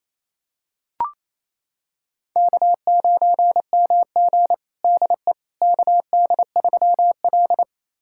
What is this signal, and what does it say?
Morse code, heard as a keyed tone, about 28 wpm, 700 Hz. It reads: K9MG DE KD3L
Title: USA Callsign DE Callsign - 4 Characters - 28wpm